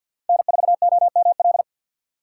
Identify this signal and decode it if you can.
Morse code, heard as a keyed tone, about 47 wpm, 700 Hz. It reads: N4OML
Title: USA Call Signs - 4x Character Spacing - 50wpm